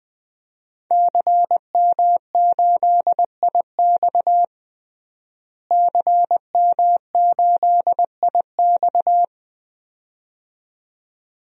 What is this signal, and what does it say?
Morse code, heard as a keyed tone, about 20 wpm, 700 Hz. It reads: CM8IX CM8IX
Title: DXCC-100 Call Signs - 5 Characters - Repeated 2x - 20wpm